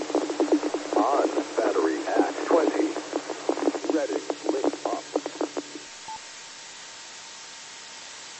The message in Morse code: none

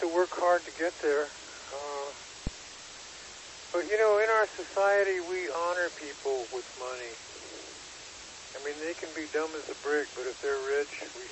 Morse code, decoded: none

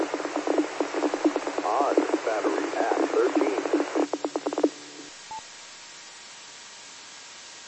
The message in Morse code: none

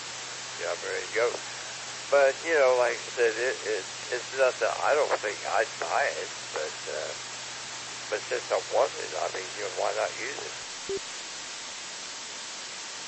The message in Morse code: none

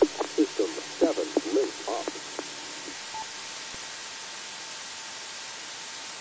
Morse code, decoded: none

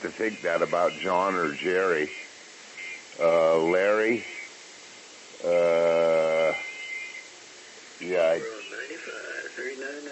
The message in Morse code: none

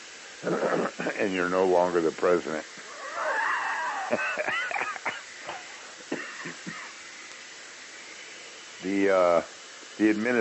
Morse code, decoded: none

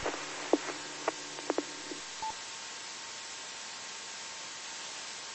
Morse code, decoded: none

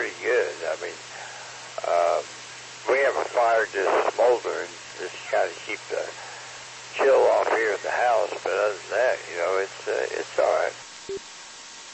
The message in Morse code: none

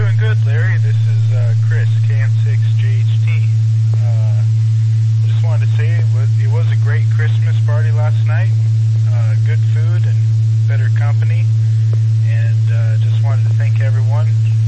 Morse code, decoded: none

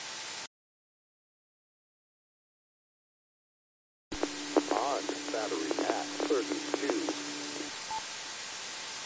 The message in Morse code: none

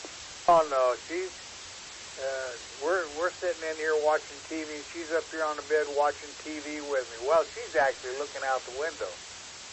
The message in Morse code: none

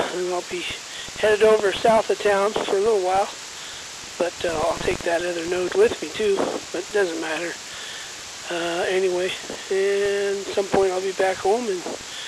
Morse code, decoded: none

